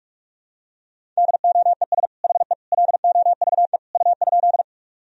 Morse code, heard as a keyed tone, about 41 wpm, 700 Hz. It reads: DOES HE LOVE U?
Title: Sentences from Top 400 Words and Abbreviations 45wpm